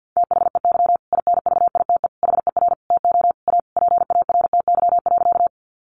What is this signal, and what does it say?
Morse code, heard as a keyed tone, about 50 wpm, 700 Hz. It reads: THEY INVITE HER TO A PARTY.
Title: Easy Phrases 50wpm